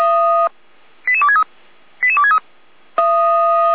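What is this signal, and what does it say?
Five-tone selective call (ZVEI), heard as quick runs of five tones, tones of about 70 ms. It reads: 88263 88263
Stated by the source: ZVEI five-tone selective call received on an SDR, fire brigade alerting.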